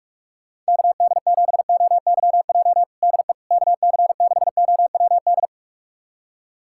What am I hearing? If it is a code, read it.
Morse code, heard as a keyed tone, about 46 wpm, 700 Hz. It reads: KD7OYJ DE KC6OWD
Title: USA Callsign DE Callsign - 6 Characters - 45wpm